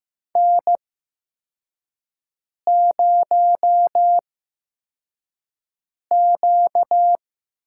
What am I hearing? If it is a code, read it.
Morse code, heard as a keyed tone, about 15 wpm, spaced as for 4 wpm, 700 Hz. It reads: N0Q
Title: USA Call Signs - 5 Characters - 8x Character Spacing - 15wpm